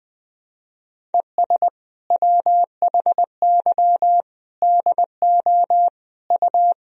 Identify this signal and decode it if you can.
Morse code, heard as a keyed tone, about 20 wpm, 700 Hz. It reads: ES WHY DO U
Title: Sentences from Top 400 Words and Abbreviations 20wpm